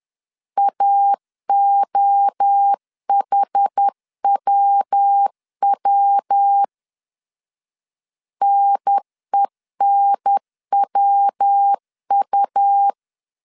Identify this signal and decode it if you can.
Morse code, heard as a keyed tone, about 10 wpm, 800 Hz. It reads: AOHWW NENWU
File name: CW / DET / 1 3 1500C S B 10wpm 3335